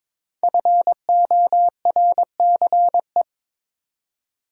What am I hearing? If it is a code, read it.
Morse code, heard as a keyed tone, about 22 wpm, 700 Hz. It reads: FORCE
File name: 5 Letter Words - First 3 Letters Given - 22wpm